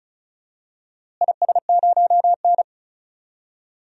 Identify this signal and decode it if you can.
Morse code, heard as a keyed tone, about 34 wpm, 700 Hz. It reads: IS0N
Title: DXCC-100 Call Signs - 4 Characters - Repeated 2x - 35wpm